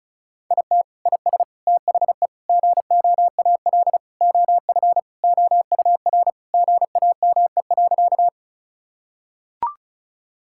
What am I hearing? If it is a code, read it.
Morse code, heard as a keyed tone, about 36 wpm, 690 Hz. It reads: IT IS THE GOAL OF OUR GAME.
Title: Sentences from Top 1000 Words - Review Part 5 - 35wpm